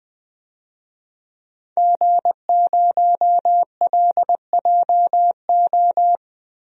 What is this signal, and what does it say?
Morse code, heard as a keyed tone, about 20 wpm, 700 Hz. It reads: G0LJO
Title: DXCC-100 Call Signs - 5 Characters - 20wpm